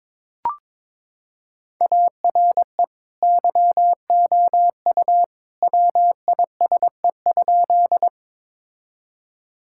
Morse code, heard as a keyed tone, about 21 wpm, 700 Hz. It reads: ARE YOU WISE?
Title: Sentences from Top 1600 Words 22wpm